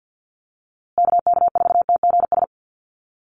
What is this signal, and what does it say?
Morse code, heard as a keyed tone, about 48 wpm, 700 Hz. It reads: KK4TGS
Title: USA Call Signs Repeated 3x 50wpm